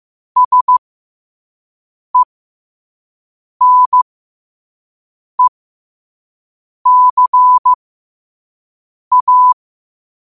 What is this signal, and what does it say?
Morse code, heard as a keyed tone, about 15 wpm, spaced as for 5 wpm, 1000 Hz. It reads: SENECA
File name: 5wpm